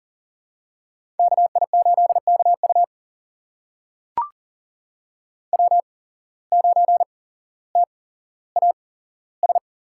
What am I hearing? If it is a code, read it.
Morse code, heard as a keyed tone, about 40 wpm, spaced as for 17 wpm, 700 Hz. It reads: KI8KU W9TAS
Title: USA Call Signs - 5 Characters - 8x Character Spacing - 40wpm